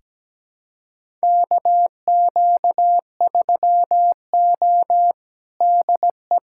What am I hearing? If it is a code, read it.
Morse code, heard as a keyed tone, about 17 wpm, 700 Hz. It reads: KQ3O DE
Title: USA Callsign DE Callsign - 4 Characters - 17wpm